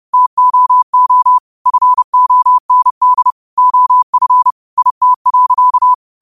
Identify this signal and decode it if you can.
Morse code, heard as a keyed tone, about 30 wpm, 1000 Hz. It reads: TOO FOND OF IT.
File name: qotdfiles / 30wpm